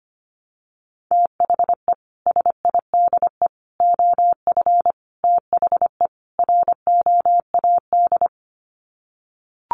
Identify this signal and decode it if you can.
Morse code, heard as a keyed tone, about 25 wpm, 700 Hz. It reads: THE SIDE OF THE ROAD
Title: Sets of 5 Words 25wpm